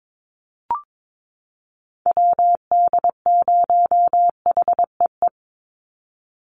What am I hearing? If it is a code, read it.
Morse code, heard as a keyed tone, about 22 wpm, 700 Hz. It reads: WD0HEE
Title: USA Call Signs 22wpm